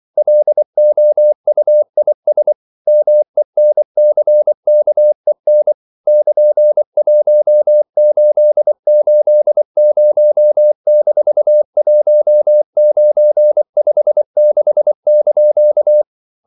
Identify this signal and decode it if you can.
Morse code, heard as a keyed tone, about 24 wpm, 600 Hz. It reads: LOUIS MENCKEN <KN>1880-1956)
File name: CW / Practice / EnglishQuotations0324